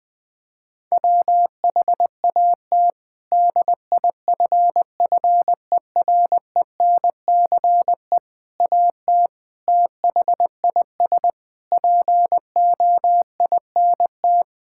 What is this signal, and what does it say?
Morse code, heard as a keyed tone, about 20 wpm, 700 Hz. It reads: WHAT DIFFERENCE AT THIS POINT
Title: Sentences with 8-Word Phrases - 20wpm